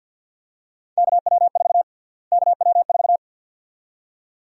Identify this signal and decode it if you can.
Morse code, heard as a keyed tone, about 50 wpm, 700 Hz. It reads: KW4 KW4
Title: USA Call Sign Prefixes - Repeated 2x - 50wpm